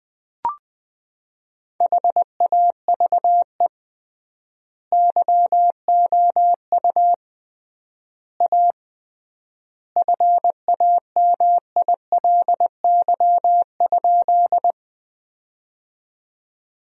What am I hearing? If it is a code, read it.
Morse code, heard as a keyed tone, about 20 wpm, 700 Hz. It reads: HAVE YOU A FAMILY?
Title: 3x Word-Spacing - Sentences from Top 400 Words - 20wpm